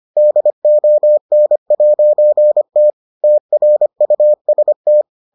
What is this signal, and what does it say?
Morse code, heard as a keyed tone, about 25 wpm, 600 Hz. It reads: DON'T TRUST